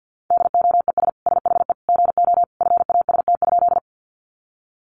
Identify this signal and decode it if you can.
Morse code, heard as a keyed tone, about 50 wpm, 700 Hz. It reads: DOES SHE GO FAST?
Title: Sentences from Top 400 Words 50wpm